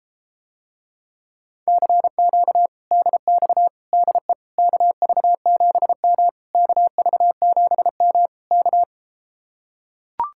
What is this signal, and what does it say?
Morse code, heard as a keyed tone, about 33 wpm, 700 Hz. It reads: CQ DX DE KV7M KV7M K